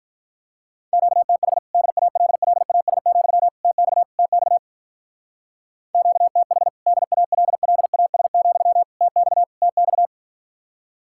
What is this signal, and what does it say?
Morse code, heard as a keyed tone, about 49 wpm, 700 Hz. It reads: QTH DALLAS, TX TX QTH DALLAS, TX TX